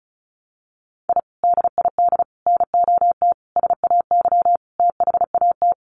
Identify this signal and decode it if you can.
Morse code, heard as a keyed tone, about 34 wpm, 700 Hz. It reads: I DID NOT SAY THAT
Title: Sets of 5 Words Encore2 35wpm